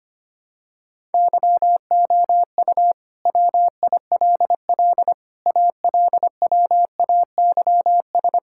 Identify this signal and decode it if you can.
Morse code, heard as a keyed tone, about 25 wpm, 700 Hz. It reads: YOU WILL ALWAYS
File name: practice / 25wpm / Sentences with 9-Word Phrases - 25wpm